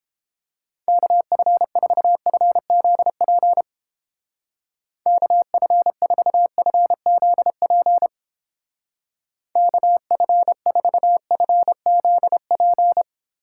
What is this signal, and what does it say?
Morse code, heard as a keyed tone, about 29 wpm, 700 Hz. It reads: KF4FZP KF4FZP KF4FZP